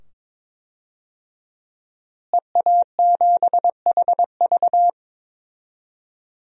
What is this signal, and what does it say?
Morse code, heard as a keyed tone, about 22 wpm, 700 Hz. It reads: EA7HV
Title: DXCC-100 Call Signs - 5 Characters - Repeated 2x - 22wpm